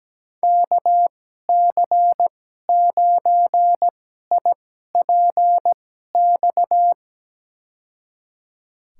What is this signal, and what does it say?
Morse code, heard as a keyed tone, about 17 wpm, spaced as for 12 wpm, 700 Hz. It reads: KC9IPX
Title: USA Call Signs - 6 Characters - 2x Character Spacing - 17wpm